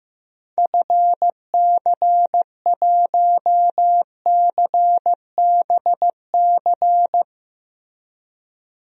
Morse code, heard as a keyed tone, about 16 wpm, 690 Hz. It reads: FC1CBC
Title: DXCC-100 Call Signs - 6 Characters - 15wpm